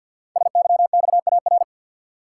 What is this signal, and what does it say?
Morse code, heard as a keyed tone, about 50 wpm, 700 Hz. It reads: SYXAR
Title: Code Groups - 5 Letters - 50wpm